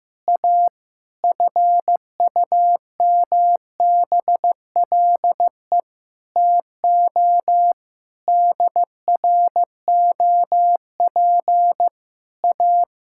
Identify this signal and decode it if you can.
Morse code, heard as a keyed tone, about 15 wpm, 700 Hz. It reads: A FUMBLE TO DROP A
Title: Pangram Challenge v2 - 15wpm